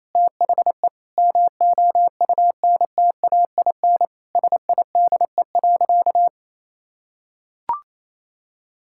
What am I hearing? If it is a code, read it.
Morse code, heard as a keyed tone, about 28 wpm, 700 Hz. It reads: THE MOUNTAIN SIDE.
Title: Sentences from Top 1000 Words - Review Part 6 - 28wpm